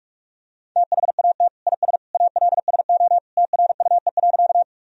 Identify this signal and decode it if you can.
Morse code, heard as a keyed tone, about 46 wpm, 700 Hz. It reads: THAT IS ALSO TRUE.
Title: Sentences from Top 1000 Words - Review Part 1 - 45wpm